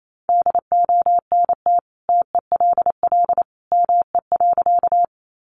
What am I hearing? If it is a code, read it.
Morse code, heard as a keyed tone, about 28 wpm, 700 Hz. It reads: DONT TELL ME.
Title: Sentences from Top 300 Words 28wpm